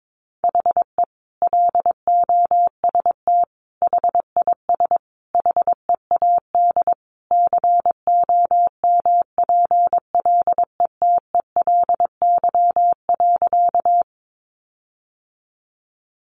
Morse code, heard as a keyed tone, about 22 wpm, 700 Hz. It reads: HE LOST HIS HEAD COMPLETELY.
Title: Sentences from Top 1400 Words 22wpm